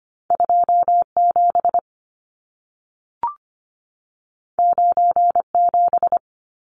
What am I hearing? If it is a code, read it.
Morse code, heard as a keyed tone, about 25 wpm, 700 Hz. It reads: 27 97
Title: Numbers - 2-Digit 25wpm